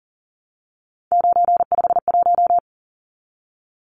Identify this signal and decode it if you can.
Morse code, heard as a keyed tone, about 40 wpm, 700 Hz. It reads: 951
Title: Numbers - 3-Digit 40wpm